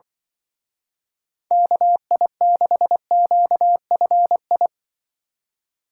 Morse code, heard as a keyed tone, about 24 wpm, 700 Hz. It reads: KI6QFI
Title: Speed Racing - USA 2x3 Call Signs 20wpm